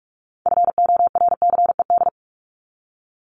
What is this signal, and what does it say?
Morse code, heard as a keyed tone, about 44 wpm, 700 Hz. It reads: FORCED
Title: ED Suffix - Top 100 Words with Base Words 45wpm